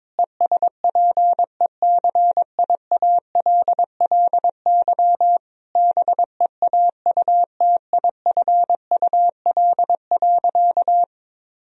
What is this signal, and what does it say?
Morse code, heard as a keyed tone, about 22 wpm, 700 Hz. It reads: ESPECIALLY BEAUTIFUL.
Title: Sentences from Top 1000 Words - Review Part 4 - 22wpm